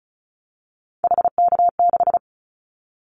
Morse code, heard as a keyed tone, about 34 wpm, 700 Hz. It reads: HK6
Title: Callsign Prefixes for Top 100 DXCC - 35wpm